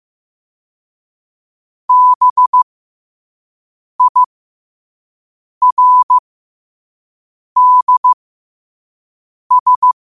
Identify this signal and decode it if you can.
Morse code, heard as a keyed tone, about 15 wpm, spaced as for 5 wpm, 1000 Hz. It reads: BIRDS
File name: qotdfiles / 5wpm